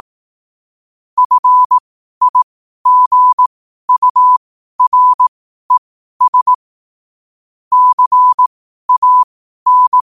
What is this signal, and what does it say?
Morse code, heard as a keyed tone, about 18 wpm, spaced as for 12 wpm, 1000 Hz. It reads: FIGURES CAN
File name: qotdfiles / 12wpm